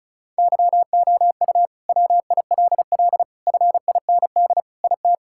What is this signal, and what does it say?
Morse code, heard as a keyed tone, about 36 wpm, 700 Hz. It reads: YOU WILL FIND IT
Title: Sentences from Top 1600 Words 35wpm